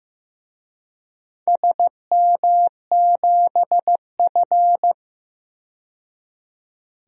Morse code, heard as a keyed tone, about 15 wpm, 700 Hz. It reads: SM7F
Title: DXCC-100 Call Signs - 4 Characters - 15wpm